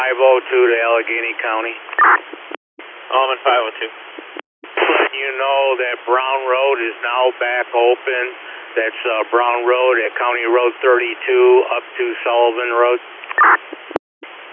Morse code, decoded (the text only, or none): none